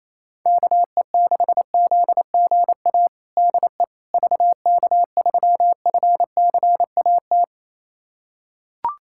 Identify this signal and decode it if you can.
Morse code, heard as a keyed tone, about 28 wpm, 700 Hz. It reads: KE6ZGA DE VK3FCAT
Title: USA Callsign DE Callsign 28wpm